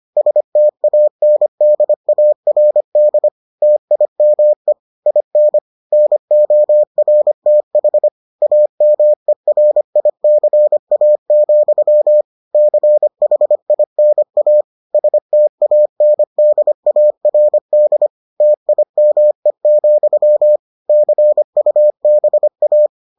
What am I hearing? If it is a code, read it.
Morse code, heard as a keyed tone, about 25 wpm, 600 Hz. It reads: STANDARD TIME IN NORTH AMERICA, CHINA STANDARD TIME, CUBA